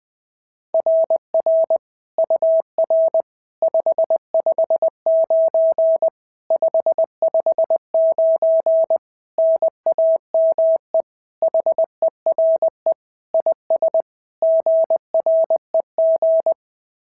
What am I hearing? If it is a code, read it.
Morse code, heard as a keyed tone, about 21 wpm, 690 Hz. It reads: RR UR 559 559 NAME HERE IS GREG
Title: SOTA QSO Practice 20wpm